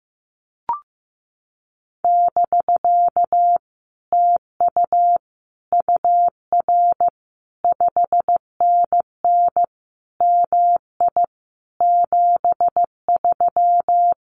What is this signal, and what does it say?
Morse code, heard as a keyed tone, about 15 wpm, 700 Hz. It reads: <BK> TU UR 5NN MI 73
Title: Common QSO Phrases Encore 15wpm